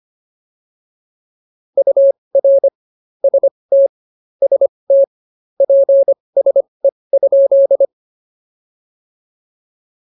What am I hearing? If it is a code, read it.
Morse code, heard as a keyed tone, about 25 wpm, spaced as for 20 wpm, 590 Hz. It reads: UR ST ST PSE?